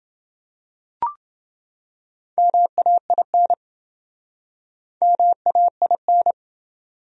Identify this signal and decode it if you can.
Morse code, heard as a keyed tone, about 28 wpm, 700 Hz. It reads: MAIN MAIN